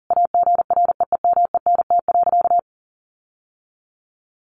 Morse code, heard as a keyed tone, about 40 wpm, 700 Hz. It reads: AGREEMENT.
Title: Sentences from Top 1500 Words 40wpm